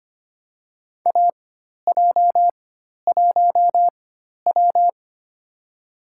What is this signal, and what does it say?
Morse code, heard as a keyed tone, about 25 wpm, spaced as for 12 wpm, 700 Hz. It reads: AJ1W
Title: USA Call Signs - 4 Characters - 4x Character Spacing - 25wpm